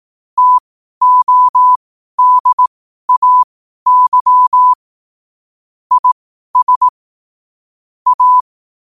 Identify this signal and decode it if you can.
Morse code, heard as a keyed tone, about 18 wpm, spaced as for 12 wpm, 1000 Hz. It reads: TODAY IS A
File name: qotdfiles / 12wpm